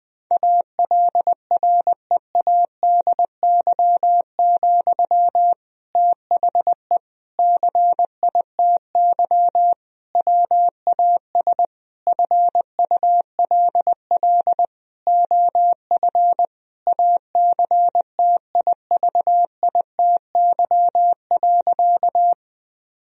Morse code, heard as a keyed tone, about 20 wpm, 700 Hz. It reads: ALREADY, THE CITY WAS FULL OF ACTIVITY.
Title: Sentences from Top 1000 Words - Review Part 4 - 20wpm